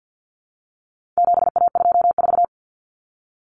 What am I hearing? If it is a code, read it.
Morse code, heard as a keyed tone, about 50 wpm, 700 Hz. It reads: 7A24
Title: Common Cut Numbers - 4-Digit 50wpm